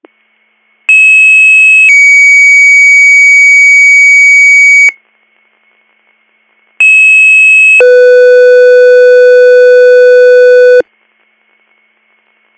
Two-tone page, heard as a one-second tone, then a three-second tone, about 2600 then 2250 Hz, about 2600 then 500 Hz. It is heard from 0.9 to 10.8 s.